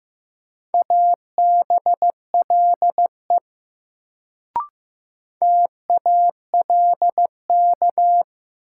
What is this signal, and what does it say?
Morse code, heard as a keyed tone, about 15 wpm, 700 Hz. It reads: ABLE TALK
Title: Sets of 2 Words - New Word Drill - 15wpm